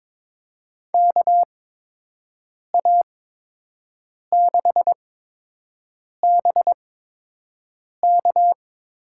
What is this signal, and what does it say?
Morse code, heard as a keyed tone, about 22 wpm, spaced as for 6 wpm, 700 Hz. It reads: KA6BK
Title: USA Call Signs - 6 Characters - 8x Character Spacing - 22wpm